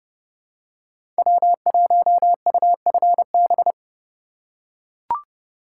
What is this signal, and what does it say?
Morse code, heard as a keyed tone, about 30 wpm, 700 Hz. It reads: W1UFB
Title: USA Call Signs - 4x Character Spacing - 30wpm